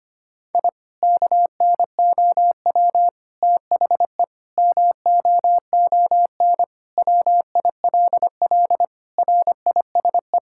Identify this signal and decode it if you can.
Morse code, heard as a keyed tone, about 25 wpm, 700 Hz. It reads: I KNOW THE MOON WILL RISE
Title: Sentences from Top 1000 Words - Review Part 6 - 25wpm